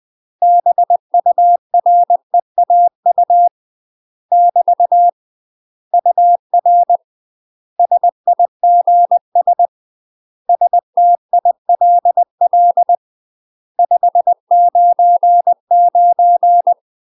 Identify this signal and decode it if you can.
Morse code, heard as a keyed tone, about 20 wpm, 700 Hz. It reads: BUREAU <BT> UR SIGS STILL 599
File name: UK MASTER QSOs TRAINING/20WPM